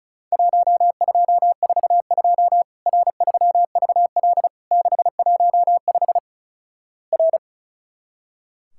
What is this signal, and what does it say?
Morse code, heard as a keyed tone, about 36 wpm, 700 Hz. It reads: 1242 R3VL 615 R
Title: Worked All Europe DX Contest - QTC Exchange - 35wpm